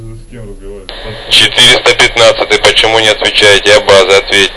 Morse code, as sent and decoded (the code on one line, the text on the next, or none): none